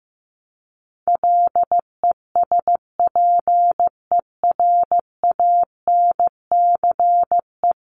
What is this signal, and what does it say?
Morse code, heard as a keyed tone, about 15 wpm, 700 Hz. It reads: LESPERANCE
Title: A Sound Of Thunder 15wpm